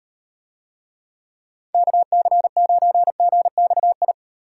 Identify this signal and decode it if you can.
Morse code, heard as a keyed tone, about 38 wpm, 700 Hz. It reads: KC9GXI